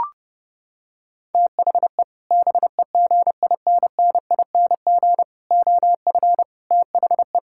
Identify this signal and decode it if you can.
Morse code, heard as a keyed tone, about 31 wpm, 700 Hz. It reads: THE BEGINNING OF THE
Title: Sentences from Top 1000 Words - Review Part 6 - 30wpm